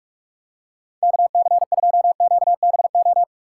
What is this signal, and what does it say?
Morse code, heard as a keyed tone, about 42 wpm, 700 Hz. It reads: KC2QBO
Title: USA Call Signs - 6 Characters - 6x Character Spacing - 45wpm